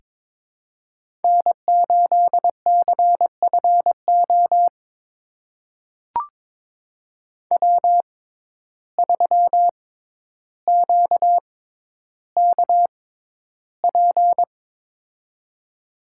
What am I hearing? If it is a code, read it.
Morse code, heard as a keyed tone, about 23 wpm, spaced as for 11 wpm, 700 Hz. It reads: N8CFO W3QKP
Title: USA Call Signs - 5 Characters - 6x Character Spacing - 22wpm